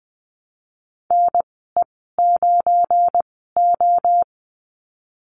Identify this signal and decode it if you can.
Morse code, heard as a keyed tone, about 20 wpm, spaced as for 15 wpm, 700 Hz. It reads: NE9O
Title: 193 Lesson 31g - Letter v - Call Signs 20wpm